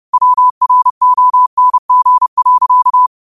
Morse code, heard as a keyed tone, about 30 wpm, 1000 Hz. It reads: WRONG.
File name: qotdfiles / 30wpm